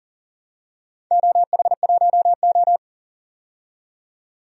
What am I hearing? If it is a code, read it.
Morse code, heard as a keyed tone, about 40 wpm, 700 Hz. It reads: OH1O
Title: DXCC-100 Call Signs - 4 Characters - 40wpm